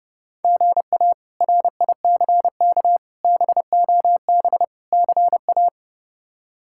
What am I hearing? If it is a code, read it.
Morse code, heard as a keyed tone, about 32 wpm, 700 Hz. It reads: GA RICK BOB CA
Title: K1USN SST QSO Practice - 30wpm